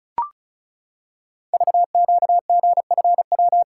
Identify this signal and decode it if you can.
Morse code, heard as a keyed tone, about 36 wpm, 700 Hz. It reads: VQGFW
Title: Code Groups - 5 Letters - 35wpm